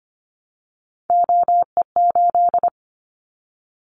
Morse code, heard as a keyed tone, about 25 wpm, 700 Hz. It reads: OE8
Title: Callsign Prefixes for Top 100 DXCC - 25wpm